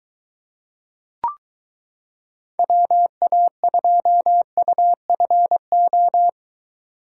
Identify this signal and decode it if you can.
Morse code, heard as a keyed tone, about 23 wpm, 700 Hz. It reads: WA2UFO